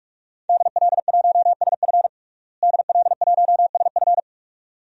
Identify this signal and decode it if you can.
Morse code, heard as a keyed tone, about 46 wpm, 700 Hz. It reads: DL1SF DL1SF
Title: DXCC-100 Call Signs - 5 Characters - Repeated 2x - 45wpm